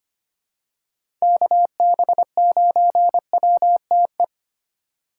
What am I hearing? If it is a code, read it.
Morse code, heard as a keyed tone, about 25 wpm, 700 Hz. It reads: KB9WTE
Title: USA Call Signs - 6 Characters - 4x Character Spacing - 25wpm